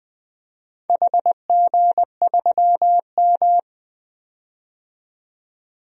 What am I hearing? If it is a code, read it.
Morse code, heard as a keyed tone, about 20 wpm, 700 Hz. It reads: HG3M
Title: DXCC-100 Call Signs - 4 Characters - 20wpm